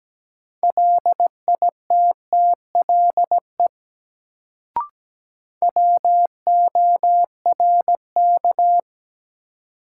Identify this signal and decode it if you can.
Morse code, heard as a keyed tone, about 17 wpm, 700 Hz. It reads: LITTLE WORK
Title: Sentences from Top 200 Words - New Word Drill - 17wpm